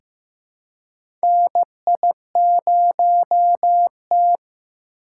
Morse code, heard as a keyed tone, about 15 wpm, 700 Hz. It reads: NI0T
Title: USA Call Signs - 4 Characters - 4x Character Spacing - 15wpm